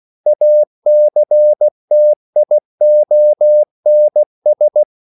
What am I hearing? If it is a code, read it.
Morse code, heard as a keyed tone, about 16 wpm, 600 Hz. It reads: ACTIONS
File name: CW / Practice / EnglishSayings16